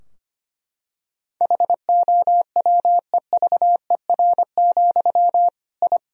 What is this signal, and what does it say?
Morse code, heard as a keyed tone, about 25 wpm, 700 Hz. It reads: HOWEVER, I
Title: Sentences from Top 1000 Words - Review Part 3 - 25wpm